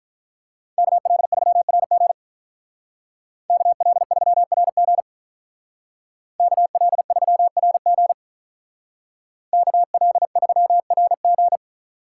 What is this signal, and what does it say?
Morse code, heard as a keyed tone, about 43 wpm, 700 Hz. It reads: KL3RG KL3RG KL3RG KL3RG
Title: Speed Racing - USA 2x2 Call Signs 35wpm